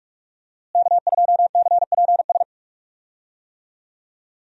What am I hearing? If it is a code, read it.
Morse code, heard as a keyed tone, about 44 wpm, 700 Hz. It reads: K2CPS
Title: USA Call Signs - 5 Characters - Repeated 3x - 45wpm